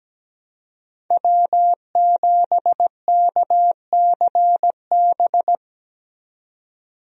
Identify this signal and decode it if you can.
Morse code, heard as a keyed tone, about 17 wpm, 700 Hz. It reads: W7KCB
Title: USA Call Signs - 5 Characters - 17wpm